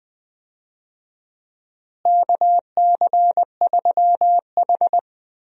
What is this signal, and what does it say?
Morse code, heard as a keyed tone, about 21 wpm, 700 Hz. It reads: KC3H
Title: USA Call Signs - 4 Characters - Repeated 2x - 20wpm